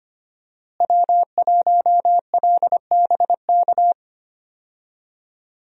Morse code, heard as a keyed tone, about 25 wpm, 700 Hz. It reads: W1LBK
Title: USA Call Signs - 5 Characters - Repeated 3x - 25wpm